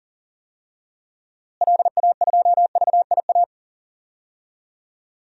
Morse code, heard as a keyed tone, about 37 wpm, 700 Hz. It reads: LA2VIA
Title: DXCC-100 Call Signs - 6 Characters - Repeated 3x - 40wpm